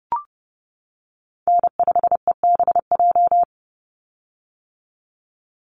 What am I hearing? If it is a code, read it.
Morse code, heard as a keyed tone, about 29 wpm, 700 Hz. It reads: N5EBJ
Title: USA Call Signs 30wpm